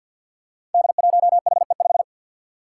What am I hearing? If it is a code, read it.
Morse code, heard as a keyed tone, about 50 wpm, 700 Hz. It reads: D1HE5
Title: Five Challenging Letter-Number Combinations 50wpm